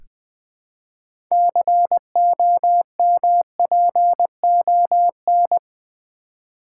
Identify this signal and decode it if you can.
Morse code, heard as a keyed tone, about 20 wpm, 700 Hz. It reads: COMPON
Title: Build A Word - 3 Letters At A Time - 9-Letter Words - 20wpm